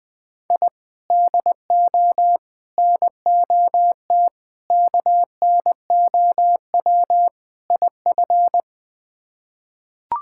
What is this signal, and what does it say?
Morse code, heard as a keyed tone, about 20 wpm, 700 Hz. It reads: I DO NOT KNOW IF
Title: Build A Phrase - 5-Word Phrases - 20wpm